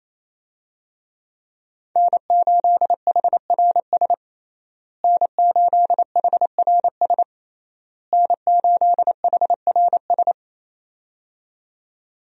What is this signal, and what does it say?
Morse code, heard as a keyed tone, about 28 wpm, 700 Hz. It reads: N8HRS N8HRS N8HRS
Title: USA Call Signs - 5 Characters - Repeated 3x - 28wpm